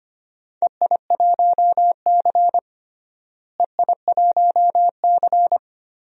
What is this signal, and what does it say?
Morse code, heard as a keyed tone, about 25 wpm, 700 Hz. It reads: EI1C EI1C
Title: DXCC-100 Call Signs - 4 Characters - Repeated 3x - 25wpm